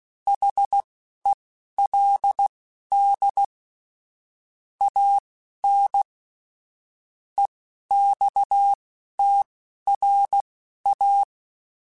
Morse code, heard as a keyed tone, about 16 wpm, spaced as for 11 wpm, 800 Hz. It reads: HELD AN EXTRA
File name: QSOs / QSOs-mp3s / kc7haa@10